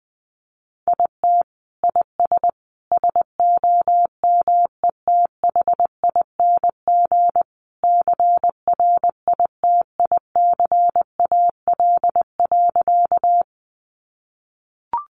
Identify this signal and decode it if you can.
Morse code, heard as a keyed tone, about 20 wpm, 700 Hz. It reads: IT IS SOMETHING CRITICAL.
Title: Sentences from Top 1100 Words 20wpm